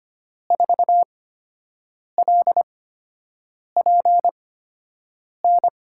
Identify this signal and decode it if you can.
Morse code, heard as a keyed tone, about 25 wpm, spaced as for 7 wpm, 700 Hz. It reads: 4LPN
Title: USA Call Signs - 5 Characters - 8x Character Spacing - 25wpm